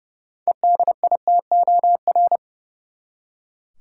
Morse code, heard as a keyed tone, about 30 wpm, 700 Hz. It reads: EDITOR
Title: Six Letter Words - 30wpm